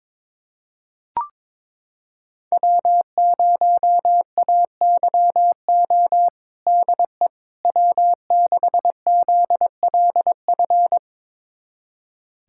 Morse code, heard as a keyed tone, about 22 wpm, 700 Hz. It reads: W0AYO DE W6ZLF
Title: USA Callsign DE Callsign - 5 Characters - 22wpm